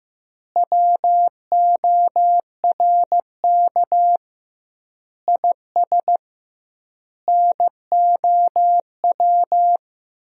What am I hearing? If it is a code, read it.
Morse code, heard as a keyed tone, about 15 wpm, 700 Hz. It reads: WORK IS NOW
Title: 2x Word-Spacing - Sentences from Top 400 Words - 15wpm